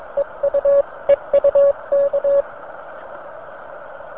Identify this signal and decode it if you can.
Morse code, heard as a keyed tone, about 22 wpm, 560 Hz. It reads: EU EUK